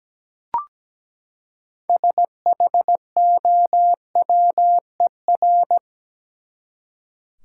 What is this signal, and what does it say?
Morse code, heard as a keyed tone, about 17 wpm, 700 Hz. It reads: SHOWER
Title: Six Letter Words - 17wpm